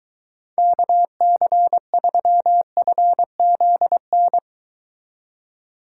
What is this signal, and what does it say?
Morse code, heard as a keyed tone, about 22 wpm, 700 Hz. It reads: KC3FZN